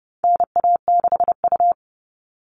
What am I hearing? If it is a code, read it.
Morse code, heard as a keyed tone, about 30 wpm, 700 Hz. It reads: NA6U
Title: All Cut Numbers - 4-Digit 30wpm